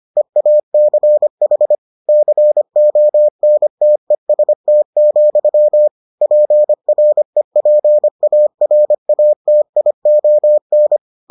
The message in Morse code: EACH CONTEST, PREPARATION

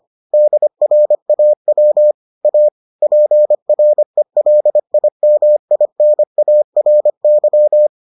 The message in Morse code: DRAW A PRELIMINARY

